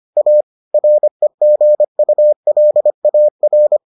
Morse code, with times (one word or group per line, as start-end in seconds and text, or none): 0.2-0.4 A
0.7-3.8 REGULAR